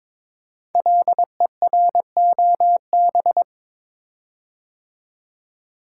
Morse code, heard as a keyed tone, about 22 wpm, 700 Hz. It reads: LEROB
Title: Code Groups - 5 Letters - 22wpm